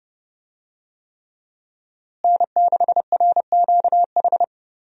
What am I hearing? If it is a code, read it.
Morse code, heard as a keyed tone, about 30 wpm, 700 Hz. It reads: N6RQH
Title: USA Call Signs - 5 Characters - Repeated 3x - 30wpm